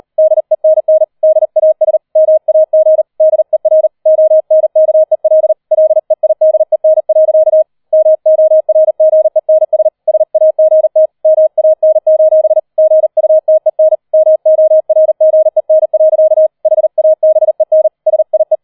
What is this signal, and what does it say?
Morse code, heard as a keyed tone, about 38 wpm, 630 Hz. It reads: DENN DAS MAG DER ONKEL LEIDEN. MORGENS SAGT MAN: GUTEN MORGEN. HABEN SIE
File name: 40wpm